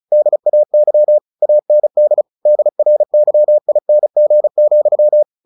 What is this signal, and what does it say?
Morse code, heard as a keyed tone, about 35 wpm, 600 Hz. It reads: DAY AND DRYING,
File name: the old man and the sea 2 35wpm